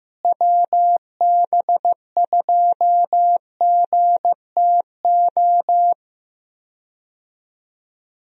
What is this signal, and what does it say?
Morse code, heard as a keyed tone, about 15 wpm, 700 Hz. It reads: WB2GTO
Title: USA Call Signs - 6 Characters - Repeated 3x - 15wpm